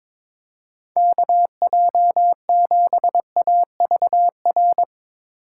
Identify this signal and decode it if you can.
Morse code, heard as a keyed tone, about 21 wpm, 700 Hz. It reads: KJ7AVR
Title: USA Call Signs - 6 Characters - 4x Character Spacing - 22wpm